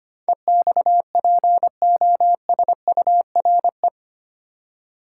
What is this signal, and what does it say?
Morse code, heard as a keyed tone, about 25 wpm, 700 Hz. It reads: EXPOSURE
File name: Eight Letter Words - 25wpm